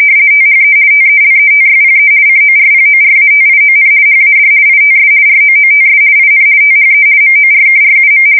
RTTY, about 50 baud. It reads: THE TELETYPE CORPORATION HAD ITS ROOTS IN THE MORKRUM
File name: fsk170-50B / teletype-corp